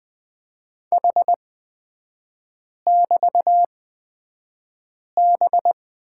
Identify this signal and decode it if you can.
Morse code, heard as a keyed tone, about 20 wpm, spaced as for 5 wpm, 700 Hz. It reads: H<BT>B
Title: 236 Lesson 38e - Letter q - Three Characters 20wpm